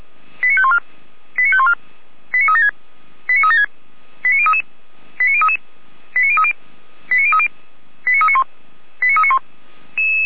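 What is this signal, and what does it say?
Five-tone selective call (ZVEI), heard as quick runs of five tones, tones of about 70 ms. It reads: 89514 89514 89376 89376 89033 89033 89030 89030 89391 89391
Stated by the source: ZVEI five-tone selective call received on an SDR.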